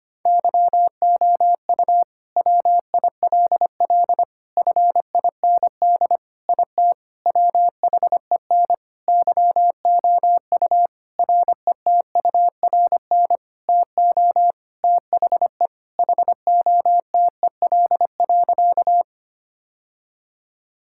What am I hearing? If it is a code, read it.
Morse code, heard as a keyed tone, about 25 wpm, 700 Hz. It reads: YOU WILL FIND IT WHEN YOU RETURN TO THE HOTEL.
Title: Sentences from Top 1600 Words 25wpm